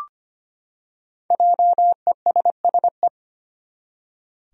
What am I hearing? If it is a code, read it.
Morse code, heard as a keyed tone, about 25 wpm, 700 Hz. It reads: JESSE
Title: USA Common Names - Max 5 Letters - 25wpm